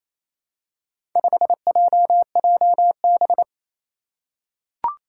Morse code, heard as a keyed tone, about 27 wpm, 700 Hz. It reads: 5JJB
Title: Four Challenging Letter-Number Combinations 28wpm